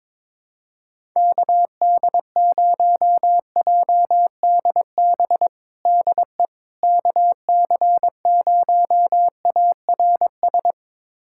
Morse code, heard as a keyed tone, about 22 wpm, 700 Hz. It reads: KD0JDB DE KC0ARS
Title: USA Callsign DE Callsign - 6 Characters - 22wpm